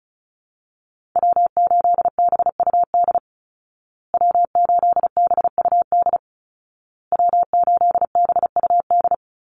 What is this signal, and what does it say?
Morse code, heard as a keyed tone, about 34 wpm, 700 Hz. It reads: W8BUD W8BUD W8BUD
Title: Call Signs Repeated 2x - 35wpm